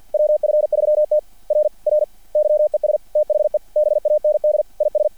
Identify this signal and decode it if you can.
Morse code, heard as a keyed tone, about 50 wpm, 600 Hz. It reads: WF3T R R YES THE BAND IS
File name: archived / CRSnet